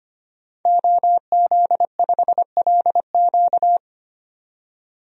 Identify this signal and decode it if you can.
Morse code, heard as a keyed tone, about 25 wpm, 700 Hz. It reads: OZ5LQ
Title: DXCC-100 Call Signs - 5 Characters - 25wpm